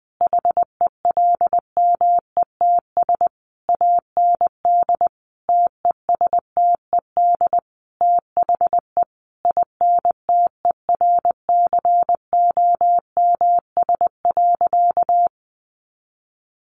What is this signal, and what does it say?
Morse code, heard as a keyed tone, about 20 wpm, 700 Hz. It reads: HELMETS AND TESTED THE INTERCOMS.
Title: A Sound Of Thunder 20wpm